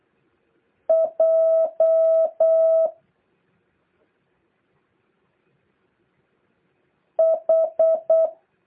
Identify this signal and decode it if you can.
Morse code, heard as a keyed tone, about 8 wpm, 650 Hz. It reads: J H
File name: jh8cbh / 2023cw